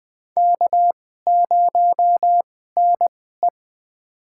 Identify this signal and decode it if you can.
Morse code, heard as a keyed tone, about 20 wpm, spaced as for 14 wpm, 700 Hz. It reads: K0NE
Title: USA Call Signs - 2x Character Spacing - 20wpm